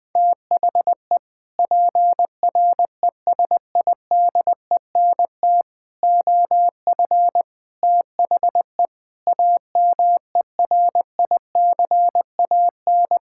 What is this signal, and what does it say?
Morse code, heard as a keyed tone, about 20 wpm, 700 Hz. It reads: THE PRESIDENT OF THE AMERICAN
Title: Sentences with 3-Word Phrases - 20wpm